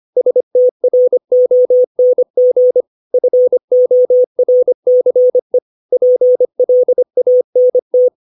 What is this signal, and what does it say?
Morse code, heard as a keyed tone, about 25 wpm, 500 Hz. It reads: STRONG FORCE PLANT
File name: 632 words / 632 Words 25WPM